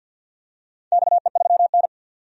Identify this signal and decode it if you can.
Morse code, heard as a keyed tone, about 47 wpm, 700 Hz. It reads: XE3N
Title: DXCC-100 Call Signs - 4 Characters - 50wpm